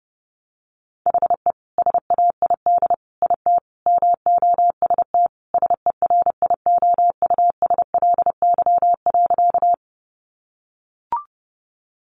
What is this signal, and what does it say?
Morse code, heard as a keyed tone, about 30 wpm, 700 Hz. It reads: HE SAID IT MOST SERIOUSLY.
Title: Sentences from Top 1200 Words 30wpm